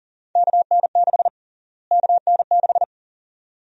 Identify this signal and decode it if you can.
Morse code, heard as a keyed tone, about 40 wpm, 700 Hz. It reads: KN6 KN6
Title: USA Call Sign Prefixes - Repeated 2x - 40wpm